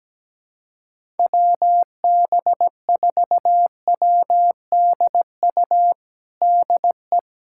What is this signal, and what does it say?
Morse code, heard as a keyed tone, about 17 wpm, 700 Hz. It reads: WB4WDU DE
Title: USA Callsign DE Callsign - 6 Characters - 17wpm